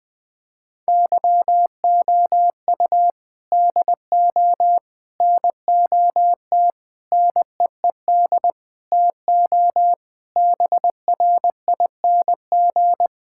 Morse code, heard as a keyed tone, about 20 wpm, 700 Hz. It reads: YOU DO NOT NEED TO BRING
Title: Sentences with 4-Word Phrases - 20wpm